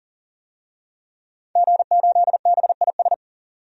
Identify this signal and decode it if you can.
Morse code, heard as a keyed tone, about 40 wpm, 700 Hz. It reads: G8BIS
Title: DXCC-100 Call Signs - 5 Characters - 40wpm